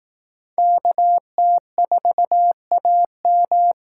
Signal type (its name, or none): Morse code